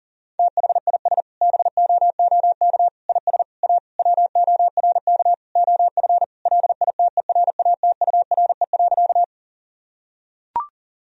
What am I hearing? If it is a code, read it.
Morse code, heard as a keyed tone, about 40 wpm, 700 Hz. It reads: THIS BOOK IS A WORK OF LITERATURE.
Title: Sentences from Top 1000 Words - Review Part 3 - 40wpm